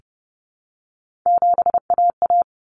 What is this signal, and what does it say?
Morse code, heard as a keyed tone, about 30 wpm, 700 Hz. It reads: 7AA
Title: Common Cut Numbers - 3-Digit 30wpm